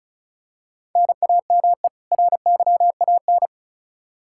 Morse code, heard as a keyed tone, about 36 wpm, 700 Hz. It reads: NAME RYAN
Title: Practice QSO - Exchange 1 - Beginning - 35wpm